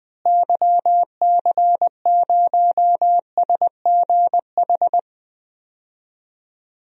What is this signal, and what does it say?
Morse code, heard as a keyed tone, about 20 wpm, 700 Hz. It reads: YC0SGH
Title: DXCC-100 Call Signs - 6 Characters - Repeated 3x - 20wpm